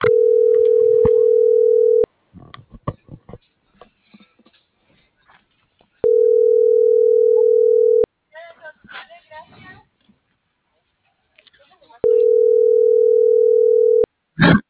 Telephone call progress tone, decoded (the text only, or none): ringback tone